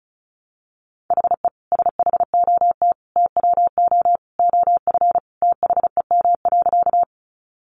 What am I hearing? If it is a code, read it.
Morse code, heard as a keyed tone, about 34 wpm, 700 Hz. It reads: HE SHOT TWO OF THEM.
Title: Sentences from Top 1100 Words 35wpm